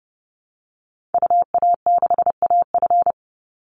Morse code, heard as a keyed tone, about 30 wpm, 700 Hz. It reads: UA6AF
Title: Call Signs Repeated 2x - 30wpm